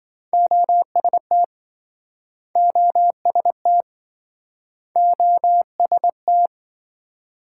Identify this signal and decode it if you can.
Morse code, heard as a keyed tone, about 27 wpm, 700 Hz. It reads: OST OST OST